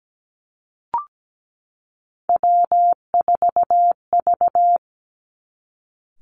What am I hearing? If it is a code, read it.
Morse code, heard as a keyed tone, about 17 wpm, 700 Hz. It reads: W4V
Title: Three Challenging Letter-Number Combinations 17wpm